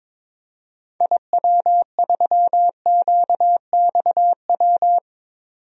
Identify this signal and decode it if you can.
Morse code, heard as a keyed tone, about 22 wpm, 700 Hz. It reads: IW3QXW
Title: DXCC-100 Call Signs - 6 Characters - 22wpm